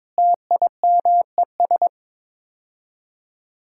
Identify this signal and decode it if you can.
Morse code, heard as a keyed tone, about 23 wpm, 700 Hz. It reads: TIMES
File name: practice / 22wpm / Build A Phrase - 5-Word Phrases - 22wpm